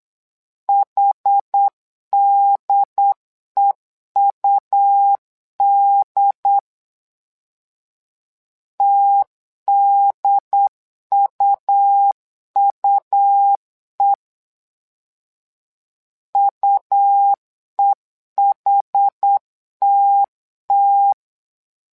Morse code, heard as a keyed tone, about 8 wpm, 800 Hz. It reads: HDEUD TDUUE UEHTT
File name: LEZIONI / 2 UHDN / 2 UHD 8wpm 2258